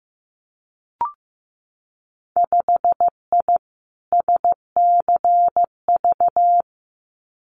Morse code, heard as a keyed tone, about 15 wpm, 700 Hz. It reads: 5I SCV
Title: Winter Field Day Exchanges Encore 15wpm